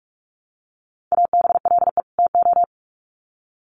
Morse code, heard as a keyed tone, about 44 wpm, 700 Hz. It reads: ABLE TO
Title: Sets of 2 Words 45wpm